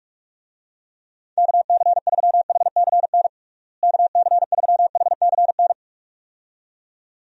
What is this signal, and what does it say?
Morse code, heard as a keyed tone, about 45 wpm, 700 Hz. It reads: KC3HCN KC3HCN
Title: USA Call Signs - 6 Characters - Repeated 2x - 45wpm